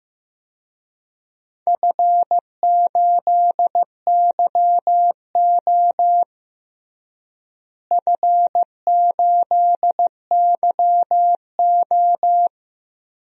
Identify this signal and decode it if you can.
Morse code, heard as a keyed tone, about 15 wpm, 700 Hz. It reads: F8YO F8YO
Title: DXCC-100 Call Signs - 4 Characters - Repeated 2x - 15wpm